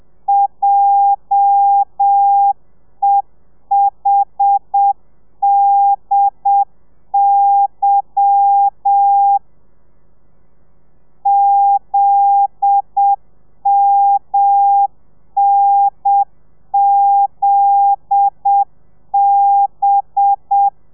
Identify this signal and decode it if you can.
Morse code, heard as a keyed tone, about 7 wpm, 800 Hz. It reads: JEHDY ZMNZB
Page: Lezione 1: audio a 30 CPM